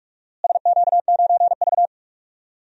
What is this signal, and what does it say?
Morse code, heard as a keyed tone, about 42 wpm, 700 Hz. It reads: SQ9V
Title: DXCC-100 Call Signs - 4 Characters - Repeated 2x - 45wpm